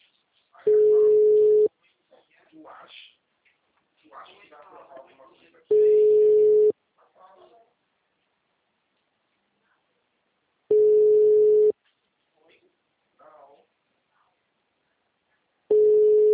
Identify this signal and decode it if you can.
Ringback tone (Europe, 425 Hz), with speech.